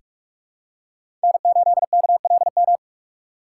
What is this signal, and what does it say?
Morse code, heard as a keyed tone, about 46 wpm, 700 Hz. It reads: N8KLM
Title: USA Call Signs - 5 Characters - 2x Character Spacing - 45wpm